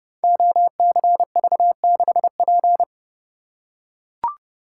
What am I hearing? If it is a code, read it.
Morse code, heard as a keyed tone, about 30 wpm, 700 Hz. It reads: OCV6P
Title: Code Groups - 5 Letter:Numbers - 30wpm